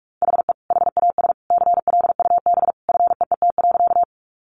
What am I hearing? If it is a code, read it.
Morse code, heard as a keyed tone, about 44 wpm, 700 Hz. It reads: HE HAS CLUB FEET.
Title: Sentences from Top 1200 Words 45wpm